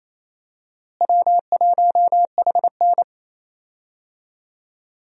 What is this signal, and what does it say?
Morse code, heard as a keyed tone, about 28 wpm, 700 Hz. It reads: W1HN
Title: USA Call Signs - 4 Characters - Repeated 3x - 28wpm